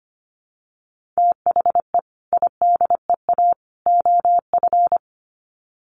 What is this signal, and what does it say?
Morse code, heard as a keyed tone, about 25 wpm, 700 Hz. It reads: THE IDEA OF
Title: Sets of 3 Words 25wpm